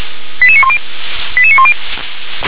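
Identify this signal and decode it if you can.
Five-tone selective call (ZVEI), heard as quick runs of five tones, tones of about 70 ms. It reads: 88910 88910
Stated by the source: ZVEI five-tone call received by SDR.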